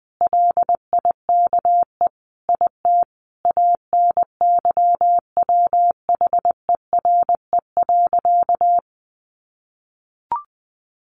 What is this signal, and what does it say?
Morse code, heard as a keyed tone, about 20 wpm, 700 Hz. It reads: LIKE IT ANYWHERE.
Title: Sentences from Top 1400 Words 20wpm